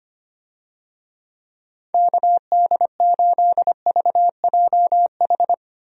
Morse code, heard as a keyed tone, about 25 wpm, 700 Hz. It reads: KD8VJH
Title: USA Call Signs - 6 Characters - Repeated 2x - 25wpm